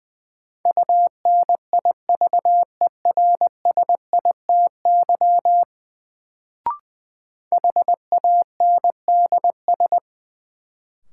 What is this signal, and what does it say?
Morse code, heard as a keyed tone, about 20 wpm, 700 Hz. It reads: UNIVERSITY HANDS
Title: Sets of 4 Words - New Word Drill - 20wpm